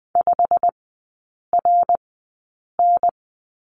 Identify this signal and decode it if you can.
Morse code, heard as a keyed tone, about 20 wpm, spaced as for 8 wpm, 690 Hz. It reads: 5RN
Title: 145 Lesson 24e - Letter y - Three Characters 20wpm